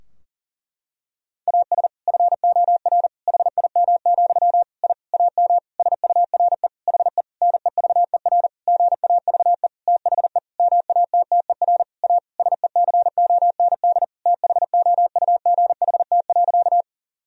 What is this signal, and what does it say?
Morse code, heard as a keyed tone, about 40 wpm, 700 Hz. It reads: AS FOR HIM, I AM SURE HE NEVER GAVE THE MATTER A SECOND THOUGHT.
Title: Sentences from Top 1000 Words - Review Part 2 - 40wpm